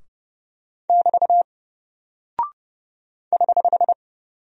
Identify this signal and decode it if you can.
Morse code, heard as a keyed tone, about 30 wpm, 700 Hz. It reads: <BT> <HH>
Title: Single Character (Letter:Number:Punctuation:Prosign) 30wpm